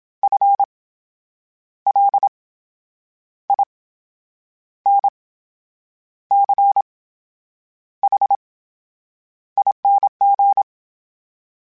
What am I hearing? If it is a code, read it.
Morse code, heard as a keyed tone, about 27 wpm, spaced as for 8 wpm, 800 Hz. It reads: FLINCHING
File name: Special INGs / Special INGs-mp3s / VerySlowToFaster Ing @25